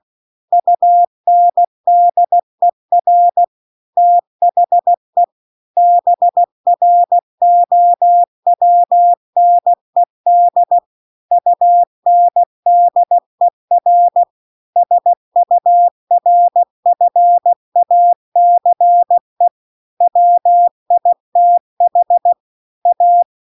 Morse code, heard as a keyed tone, about 16 wpm, 700 Hz. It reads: UNDER THE BROWNED UNDER SURFACE WITH A
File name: CW-The-Big-Twoheaded-River-2-16WPM